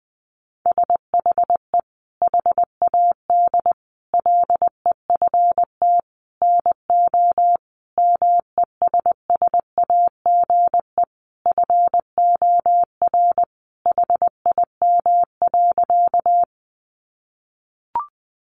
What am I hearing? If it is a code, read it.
Morse code, heard as a keyed tone, about 20 wpm, 700 Hz. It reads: SHE HAD LEFT NO MESSAGE FOR HIM.
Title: Sentences from Top 1100 Words 20wpm